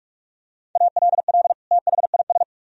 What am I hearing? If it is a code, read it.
Morse code, heard as a keyed tone, about 46 wpm, 700 Hz. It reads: ALL THIS
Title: Sentences from Top 1000 Words - Review Part 4 - 45wpm